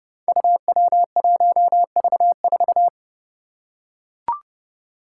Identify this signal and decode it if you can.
Morse code, heard as a keyed tone, about 30 wpm, 700 Hz. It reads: UW1V4
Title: Five Challenging Letter-Number Combinations 30wpm